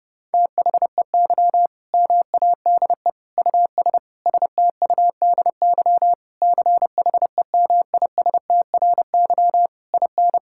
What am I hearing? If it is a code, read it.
Morse code, heard as a keyed tone, about 30 wpm, 700 Hz. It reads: THEY MADE US STUDY CHEMISTRY IN
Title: Sentences from Top 1000 Words - Review Part 6 - 30wpm